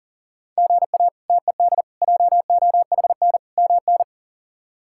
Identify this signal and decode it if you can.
Morse code, heard as a keyed tone, about 40 wpm, 700 Hz. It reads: GA TED JOHN MN
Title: K1USN SST QSO Practice - 40wpm